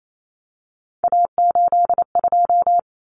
Morse code, heard as a keyed tone, about 28 wpm, 700 Hz. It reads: A82
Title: Common Cut Numbers - 3-Digit 28wpm